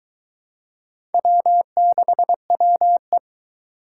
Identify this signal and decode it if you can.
Morse code, heard as a keyed tone, about 23 wpm, 700 Hz. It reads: W6WE